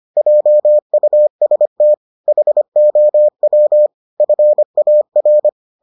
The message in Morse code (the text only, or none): JUST HOW FAR